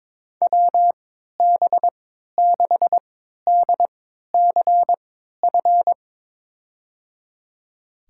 Morse code, heard as a keyed tone, about 22 wpm, spaced as for 12 wpm, 700 Hz. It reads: WB6DCF
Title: USA Call Signs - 6 Characters - 3x Character Spacing - 22wpm